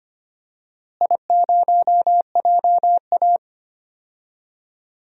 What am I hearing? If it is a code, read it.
Morse code, heard as a keyed tone, about 25 wpm, 700 Hz. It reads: I0JA
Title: DXCC-100 Call Signs - 4 Characters - Repeated 3x - 25wpm